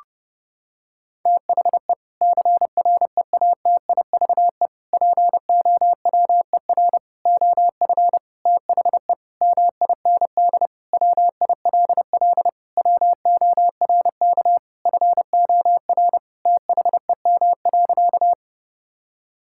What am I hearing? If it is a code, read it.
Morse code, heard as a keyed tone, about 30 wpm, 700 Hz. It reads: THE CREATIVE POWER OF THE MIND WILL WORK FOR THEM.
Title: Sentences from Top 1000 Words - Review Part 5 - 30wpm